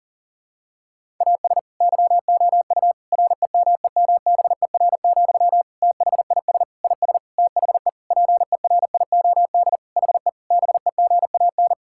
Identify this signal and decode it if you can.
Morse code, heard as a keyed tone, about 40 wpm, 700 Hz. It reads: AS YOU REMEMBER, THIS IS THE PERIOD HE BEGAN
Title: Sentences from Top 1000 Words - Review Part 5 - 40wpm